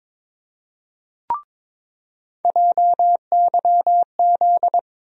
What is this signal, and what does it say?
Morse code, heard as a keyed tone, about 22 wpm, 700 Hz. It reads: JYZ
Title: USA Call Sign Postfixes - 3 Letters 22wpm